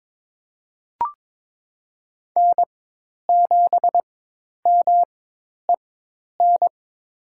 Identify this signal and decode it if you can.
Morse code, heard as a keyed tone, about 22 wpm, spaced as for 10 wpm, 700 Hz. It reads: N7MEN
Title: USA Call Signs - 4x Character Spacing - 22wpm